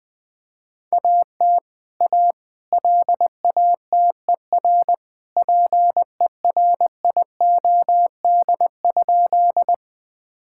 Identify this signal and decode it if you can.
Morse code, heard as a keyed tone, about 20 wpm, 700 Hz. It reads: AT A LATER PERIOD?
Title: Sentences from Top 1000 Words - Review Part 5 - 20wpm